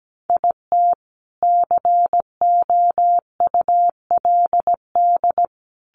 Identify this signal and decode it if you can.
Morse code, heard as a keyed tone, about 17 wpm, 700 Hz. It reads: IT COULD
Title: Sentences from Top 1200 Words 17wpm